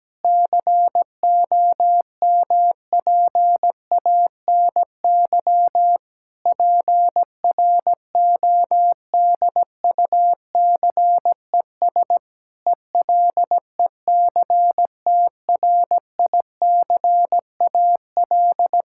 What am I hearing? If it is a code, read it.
Morse code, heard as a keyed tone, about 17 wpm, 700 Hz. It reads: COMPANY PRODUCES ELECTRICAL
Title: Sentences from Top 1000 Words - Review Part 6 - 17wpm